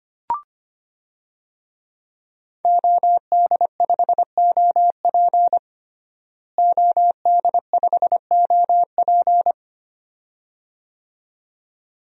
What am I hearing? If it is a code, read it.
Morse code, heard as a keyed tone, about 25 wpm, 700 Hz. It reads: OD5OP OD5OP
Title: DXCC-100 Call Signs - 5 Characters - Repeated 2x - 25wpm